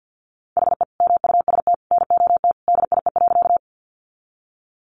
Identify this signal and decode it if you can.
Morse code, heard as a keyed tone, about 50 wpm, 700 Hz. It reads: HE MUST NOT DIE.
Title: Sentences from Top 1300 Words 50wpm